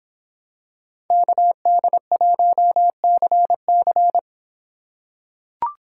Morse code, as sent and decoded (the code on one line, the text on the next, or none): -.- -.. .---- -.-. -.-.
KD1CC